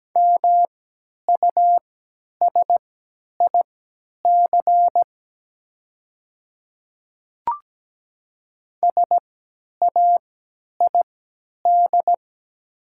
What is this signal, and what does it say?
Morse code, heard as a keyed tone, about 17 wpm, spaced as for 10 wpm, 700 Hz. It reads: MUSIC SAID
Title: 3x Character Spacing - Top 500 Words - 17wpm